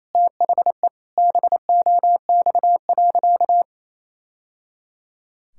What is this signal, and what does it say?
Morse code, heard as a keyed tone, about 28 wpm, 700 Hz. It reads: THE BOX.
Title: Sentences from Top 1000 Words - Review Part 5 - 28wpm